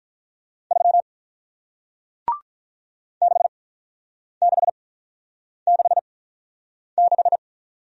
Morse code, heard as a keyed tone, about 45 wpm, 700 Hz. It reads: 3 6 6 6 6